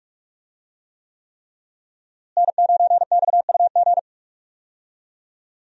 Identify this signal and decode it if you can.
Morse code, heard as a keyed tone, about 46 wpm, 700 Hz. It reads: N9XUG
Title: USA Call Signs - 5 Characters - Repeated 3x - 45wpm